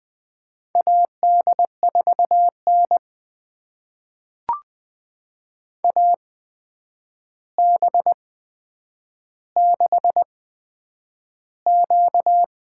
Morse code, heard as a keyed tone, about 20 wpm, spaced as for 9 wpm, 700 Hz. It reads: AD4N AB6Q
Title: USA Call Signs - 4 Characters - 8x Character Spacing - 20wpm